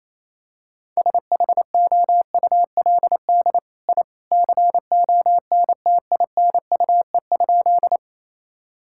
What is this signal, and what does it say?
Morse code, heard as a keyed tone, about 28 wpm, 700 Hz. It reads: SHOULD I CONTINUE?
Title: Sentences from Top 1000 Words - Review Part 6 - 28wpm